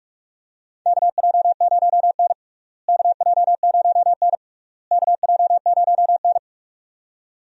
Morse code, heard as a keyed tone, about 46 wpm, 700 Hz. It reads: KJ0N KJ0N KJ0N
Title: USA Call Signs - 4 Characters - Repeated 3x - 45wpm